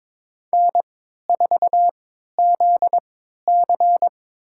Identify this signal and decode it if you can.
Morse code, heard as a keyed tone, about 22 wpm, spaced as for 13 wpm, 700 Hz. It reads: N4ZC
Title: USA Call Signs - 4 Characters - 3x Character Spacing - 22wpm